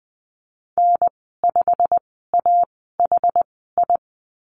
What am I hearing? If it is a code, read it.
Morse code, heard as a keyed tone, about 20 wpm, spaced as for 14 wpm, 700 Hz. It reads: N5AHI
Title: 161 Lesson 26g - Letter g - Call Signs 20wpm